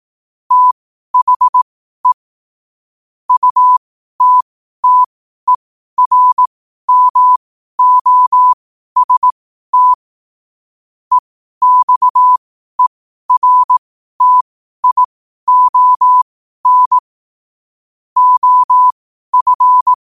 Morse code, heard as a keyed tone, about 18 wpm, spaced as for 12 wpm, 1000 Hz. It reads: THE UTTERMOST EXERTION OF
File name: qotdfiles / 12wpm